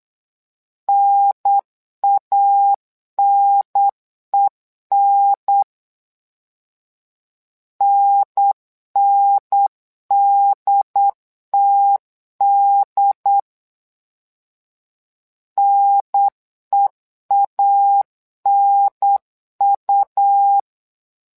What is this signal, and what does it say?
Morse code, heard as a keyed tone, about 8 wpm, 800 Hz. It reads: NANEN NNDTD NEANU
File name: LEZIONI / 2 UHDN / 2 UHDN 8wpm2232